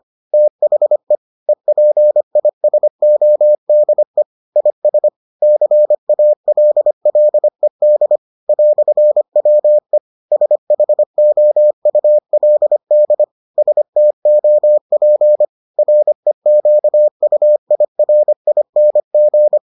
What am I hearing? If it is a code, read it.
Morse code, heard as a keyed tone, about 25 wpm, 600 Hz. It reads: THE EPISODE IS CALLED "WE SHOULD STOP REQUIRING